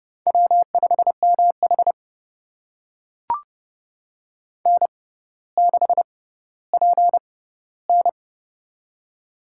Morse code, heard as a keyed tone, about 30 wpm, spaced as for 15 wpm, 700 Hz. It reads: W5MH N6PN
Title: USA Call Signs - 4 Characters - 6x Character Spacing - 30wpm